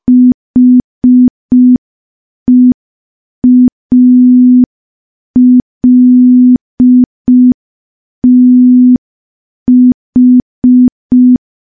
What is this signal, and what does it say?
Morse code, heard as a keyed tone, about 5 wpm, 250 Hz. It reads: HEALTH
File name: morse / morse-training-05WPM